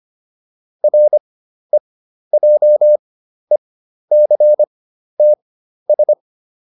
Morse code, heard as a keyed tone, about 25 wpm, spaced as for 12 wpm, 600 Hz. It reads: REJECTS